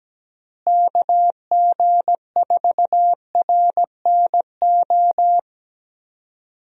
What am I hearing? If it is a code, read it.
Morse code, heard as a keyed tone, about 17 wpm, 700 Hz. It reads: KG4RNO
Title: USA Call Signs - 6 Characters - Repeated 3x - 17wpm